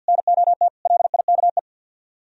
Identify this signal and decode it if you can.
Morse code, heard as a keyed tone, about 65 wpm, spaced as for 57 wpm, 700 Hz. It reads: NOT LIKE
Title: Speed-Racing - Sets of 5 Words 50wpm